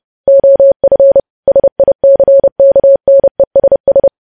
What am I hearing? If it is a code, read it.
Morse code, heard as a keyed tone, about 30 wpm, 550 Hz. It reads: OF SICKNESS
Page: news headlines in morse code at 30 wpm